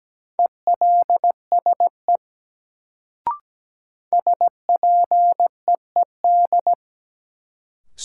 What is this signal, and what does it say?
Morse code, heard as a keyed tone, about 17 wpm, 700 Hz. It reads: ELSE SPEED
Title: Sentences from Top 800 Words - New Word Drill - 17wpm